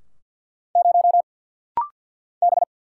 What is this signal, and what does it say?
Morse code, heard as a keyed tone, about 48 wpm, 700 Hz. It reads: 0 B
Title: Single Letter-Number - Rapid-Fire 50wpm